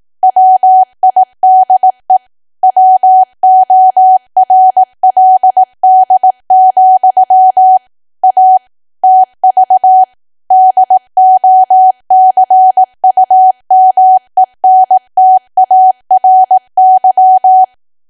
Morse code, heard as a keyed tone, about 18 wpm, 750 Hz. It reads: WIDE WORLD, A TV DOCUMENTARY
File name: Archive / 18 WPM / 18WPM